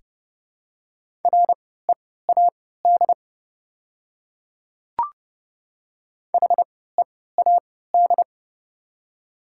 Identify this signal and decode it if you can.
Morse code, heard as a keyed tone, about 29 wpm, spaced as for 17 wpm, 700 Hz. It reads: READ HEAD
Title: 3x Character Spacing - Top 300 Words - 30wpm